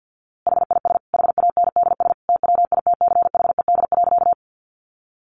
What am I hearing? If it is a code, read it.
Morse code, heard as a keyed tone, about 50 wpm, 700 Hz. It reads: HIS HANDS TWITCHED.
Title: A Sound Of Thunder 50wpm